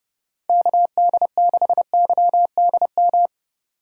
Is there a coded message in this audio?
Yes: Morse code from 0.5 to 3.3 s, heard as a keyed tone, about 29 wpm, 700 Hz. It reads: KD6YDM